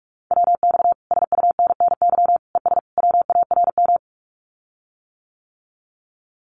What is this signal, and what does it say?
Morse code, heard as a keyed tone, about 44 wpm, 700 Hz. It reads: WX SUNNY ES WARM
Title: Common QSO Phrases Encore 45wpm